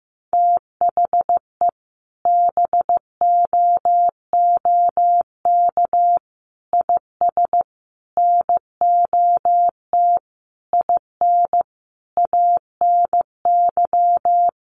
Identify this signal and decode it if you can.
Morse code, heard as a keyed tone, about 15 wpm, 700 Hz. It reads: THE BOOK IS NOT IN ANY
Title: Sentences from Top 1300 Words 15wpm